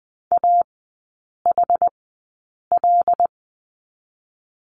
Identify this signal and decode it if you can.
Morse code, heard as a keyed tone, about 20 wpm, spaced as for 8 wpm, 700 Hz. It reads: AHL
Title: 121 Lesson 20e - Number 3 - Three Characters 20wpm